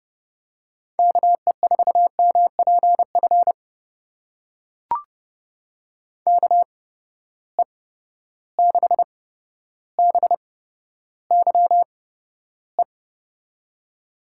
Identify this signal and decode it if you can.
Morse code, heard as a keyed tone, about 31 wpm, spaced as for 13 wpm, 700 Hz. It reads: KE4MPF KE6BYE
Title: USA Call Signs - 6 Characters - 8x Character Spacing - 30wpm